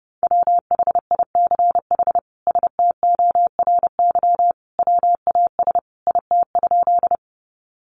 Morse code, heard as a keyed tone, about 30 wpm, 700 Hz. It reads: WHICH STORY WAS IT?
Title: Sentences from Top 300 Words 30wpm